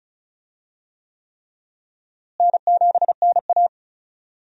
Morse code, heard as a keyed tone, about 34 wpm, 700 Hz. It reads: N7NA
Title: USA Call Signs - 4 Characters - Repeated 3x - 35wpm